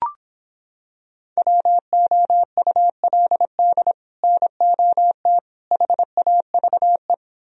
Morse code, heard as a keyed tone, about 26 wpm, 700 Hz. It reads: WOULD NOT HAVE